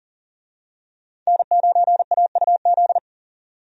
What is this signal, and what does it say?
Morse code, heard as a keyed tone, about 40 wpm, 700 Hz. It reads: N9AUZ
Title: USA Call Signs - 5 Characters - 40wpm